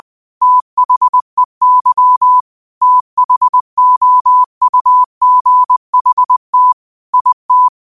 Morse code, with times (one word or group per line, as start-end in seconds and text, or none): 0.4-2.4 THEY
2.8-6.7 THOUGHT
7.1-7.7 IT